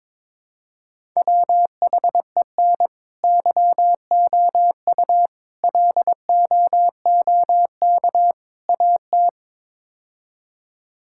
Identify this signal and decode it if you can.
Morse code, heard as a keyed tone, about 22 wpm, 700 Hz. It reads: WHEN YOU LOOK AT
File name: practice / 22wpm / Build A Phrase - 5-Word Phrases - 22wpm